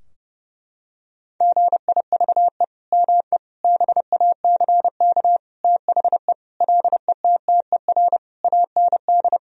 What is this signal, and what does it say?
Morse code, heard as a keyed tone, about 30 wpm, 700 Hz. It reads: GIVE ME BACK THE LETTER AND
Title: Sentences from Top 1000 Words - Review Part 4 - 30wpm